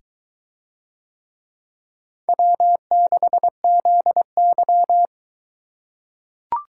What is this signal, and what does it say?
Morse code, heard as a keyed tone, about 22 wpm, 700 Hz. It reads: W6ZY